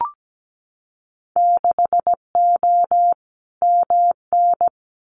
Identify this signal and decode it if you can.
Morse code, heard as a keyed tone, about 17 wpm, 700 Hz. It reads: 6O MN
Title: Winter Field Day Exchanges Encore 17wpm